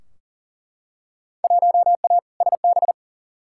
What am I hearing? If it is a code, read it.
Morse code, heard as a keyed tone, about 39 wpm, 700 Hz. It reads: 1A SB
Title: Field Day Exchanges Encore - 40wpm